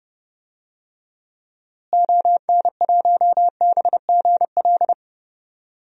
Morse code, heard as a keyed tone, about 30 wpm, 700 Hz. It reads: ON1BGL
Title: DXCC-100 Call Signs - 6 Characters - Repeated 2x - 30wpm